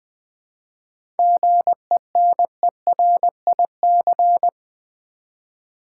Morse code, heard as a keyed tone, about 20 wpm, 700 Hz. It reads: GENERIC
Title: 7 Letter Words - First 3 Letters Given - 20wpm